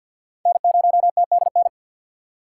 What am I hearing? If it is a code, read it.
Morse code, heard as a keyed tone, about 51 wpm, 700 Hz. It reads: N0TDN
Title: USA Call Signs - 5 Characters - 2x Character Spacing - 50wpm